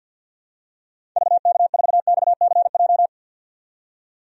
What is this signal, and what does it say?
Morse code, heard as a keyed tone, about 52 wpm, 700 Hz. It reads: VK4XCJ
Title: DXCC-100 Call Signs - 6 Characters - 50wpm